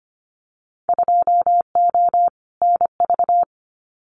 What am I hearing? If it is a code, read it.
Morse code, heard as a keyed tone, about 25 wpm, 700 Hz. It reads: 2O NV
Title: Winter Field Day Exchanges 25wpm